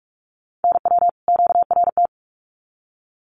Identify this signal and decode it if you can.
Morse code, heard as a keyed tone, about 44 wpm, 700 Hz. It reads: NW QRT
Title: Common QSO Phrases 45wpm